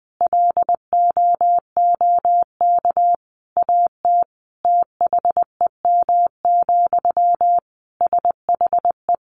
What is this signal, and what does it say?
Morse code, heard as a keyed tone, about 20 wpm, 700 Hz. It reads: LOOK AT THEM, SHE
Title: Sentences from Top 1400 Words 20wpm